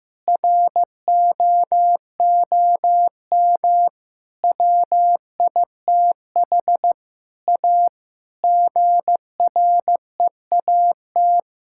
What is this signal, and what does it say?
Morse code, heard as a keyed tone, about 15 wpm, 700 Hz. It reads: ROOM WITH A GREAT
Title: Sentences from Top 1600 Words 15wpm